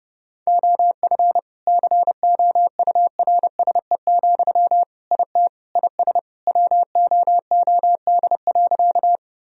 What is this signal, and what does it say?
Morse code, heard as a keyed tone, about 30 wpm, 700 Hz. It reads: OF COURSE, IT IS WOOD.
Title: Sentences from Top 1000 Words - Review Part 2 - 30wpm